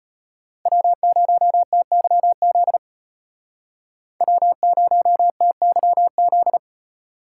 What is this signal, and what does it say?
Morse code, heard as a keyed tone, about 35 wpm, 700 Hz. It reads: W0TYZ W0TYZ